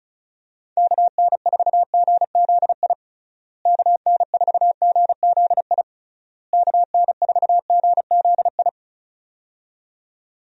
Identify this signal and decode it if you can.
Morse code, heard as a keyed tone, about 34 wpm, 700 Hz. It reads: KN4GZI KN4GZI KN4GZI
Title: USA Call Signs - 6 Characters - Repeated 3x - 35wpm